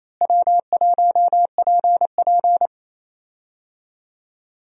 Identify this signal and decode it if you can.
Morse code, heard as a keyed tone, about 28 wpm, 700 Hz. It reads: W1PP
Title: USA Call Signs - 4 Characters - Repeated 2x - 28wpm